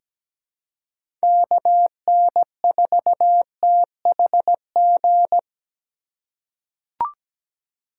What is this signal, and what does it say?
Morse code, heard as a keyed tone, about 17 wpm, 700 Hz. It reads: KN4THG
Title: USA Call Signs - 6 Characters - 3x Character Spacing - 17wpm